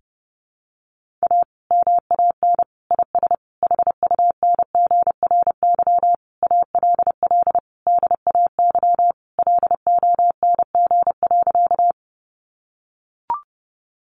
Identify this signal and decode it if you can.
Morse code, heard as a keyed tone, about 28 wpm, 700 Hz. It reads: A MAN IS HUNGRY ALL DAY LONG.
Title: Sentences from Top 1400 Words 30wpm